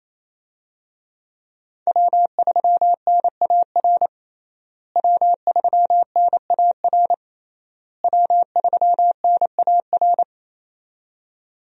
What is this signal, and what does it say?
Morse code, heard as a keyed tone, about 28 wpm, 700 Hz. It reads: W3NAR W3NAR W3NAR
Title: USA Call Signs - 5 Characters - Repeated 3x - 28wpm